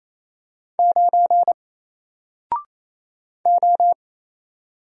Morse code, heard as a keyed tone, about 27 wpm, 700 Hz. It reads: 9 O
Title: Single Letter-Number Alt 28wpm